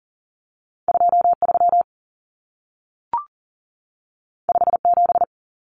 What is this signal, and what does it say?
Morse code, heard as a keyed tone, about 40 wpm, 700 Hz. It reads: 23 57
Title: Common Cut Numbers - 2-Digit 40wpm